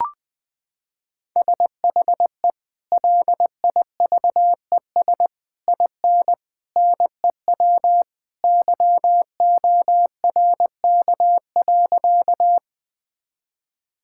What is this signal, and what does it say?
Morse code, heard as a keyed tone, about 21 wpm, 700 Hz. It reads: SHE LIVES IN NEW YORK.
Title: Sentences with 3-Word Phrases - 20wpm